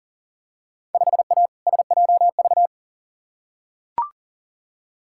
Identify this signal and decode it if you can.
Morse code, heard as a keyed tone, about 40 wpm, 700 Hz. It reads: 5A SJV
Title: Field Day Exchanges - 40wpm